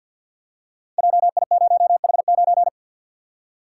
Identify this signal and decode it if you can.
Morse code, heard as a keyed tone, about 47 wpm, 700 Hz. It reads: JI0H9
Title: Code Groups - 5 Letter:Numbers - 50wpm